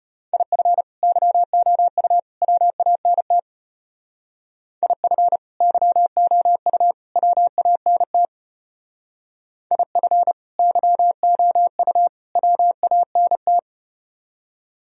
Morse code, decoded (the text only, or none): IF YOU WANT IF YOU WANT IF YOU WANT